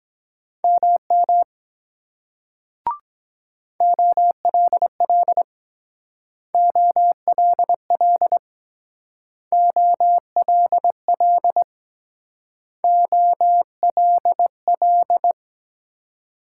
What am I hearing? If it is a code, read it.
Morse code, heard as a keyed tone, about 21 wpm, 700 Hz. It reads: MM OLL OLL OLL OLL
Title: Speed-Racing Multi-letter Phonemes - 17wpm